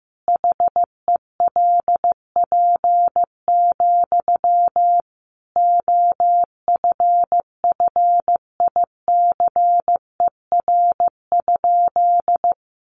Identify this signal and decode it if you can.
Morse code, heard as a keyed tone, about 15 wpm, 700 Hz. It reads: HELP, OFFICER?
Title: Sentences from Top 1200 Words 15wpm